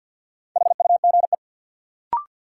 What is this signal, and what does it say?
Morse code, heard as a keyed tone, about 48 wpm, 700 Hz. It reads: HUGE
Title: Sentences from Top 1100 Words - New Word Drill - 50wpm